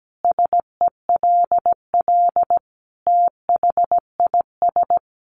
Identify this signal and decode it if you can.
Morse code, heard as a keyed tone, about 17 wpm, 700 Hz. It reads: SELL THIS
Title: Sentences from Top 1200 Words 17wpm